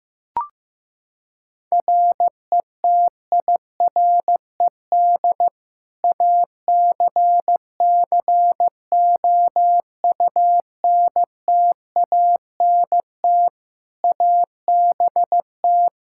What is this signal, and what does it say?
Morse code, heard as a keyed tone, about 15 wpm, 700 Hz. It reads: RETIRED ACCOUNTANT ABT
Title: Practice QSO - Exchange 3 - More Info - 15wpm